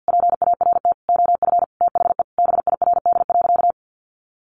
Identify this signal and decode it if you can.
Morse code, heard as a keyed tone, about 50 wpm, 700 Hz. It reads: PART OF THE BIRD.
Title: Sentences from Top 1500 Words 50wpm